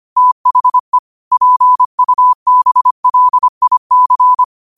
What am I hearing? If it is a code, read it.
Morse code, heard as a keyed tone, about 25 wpm, 1000 Hz. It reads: THE PUBLIC
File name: qotdfiles / 25wpm